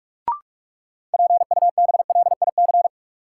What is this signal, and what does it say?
Morse code, heard as a keyed tone, about 46 wpm, 700 Hz. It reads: PUBLIC
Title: Sentences from Top 600 Words - New Word Drill - 45wpm